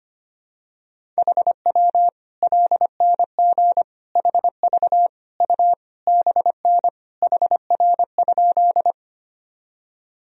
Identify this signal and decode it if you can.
Morse code, heard as a keyed tone, about 25 wpm, 700 Hz. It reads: HW LNG HV U BN HR?
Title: Sentences from Top 200 Words and Abbreviations 25wpm